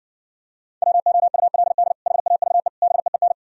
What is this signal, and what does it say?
Morse code, heard as a keyed tone, about 68 wpm, 700 Hz. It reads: WOULD HAVE BEEN